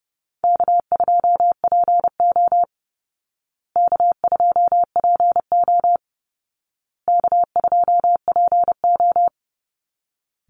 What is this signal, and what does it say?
Morse code, heard as a keyed tone, about 30 wpm, 700 Hz. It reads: K2PO K2PO K2PO
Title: Call Signs Repeated 2x - 30wpm